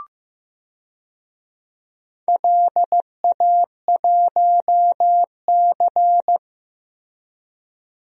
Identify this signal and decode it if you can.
Morse code, heard as a keyed tone, about 15 wpm, 700 Hz. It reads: LA1C
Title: DXCC-100 Call Signs - 4 Characters - Repeated 3x - 15wpm